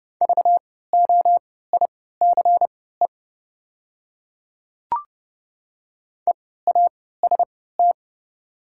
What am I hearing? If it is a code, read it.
Morse code, heard as a keyed tone, about 30 wpm, spaced as for 17 wpm, 700 Hz. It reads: VOICE EAST
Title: 3x Character Spacing - Top 500 Words - 30wpm